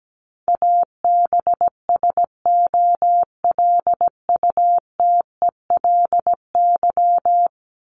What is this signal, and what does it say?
Morse code, heard as a keyed tone, about 17 wpm, 700 Hz. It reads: ABSOLUTELY
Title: LY Suffix - Top 100 Words with Base Words 17wpm